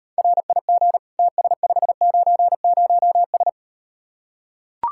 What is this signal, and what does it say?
Morse code, heard as a keyed tone, about 38 wpm, 700 Hz. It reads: RIG TS590S